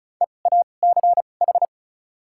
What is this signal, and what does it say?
Morse code, heard as a keyed tone, about 36 wpm, spaced as for 25 wpm, 700 Hz. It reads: EACH
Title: 2x Character Spacing - Top 100 Words - 35wpm